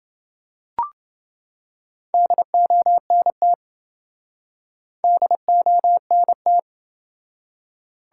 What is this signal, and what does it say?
Morse code, heard as a keyed tone, about 27 wpm, 700 Hz. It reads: DONT DONT